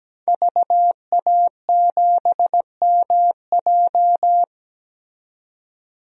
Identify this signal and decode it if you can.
Morse code, heard as a keyed tone, about 17 wpm, 700 Hz. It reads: VA7MJ
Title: DXCC-100 Call Signs - 5 Characters - 17wpm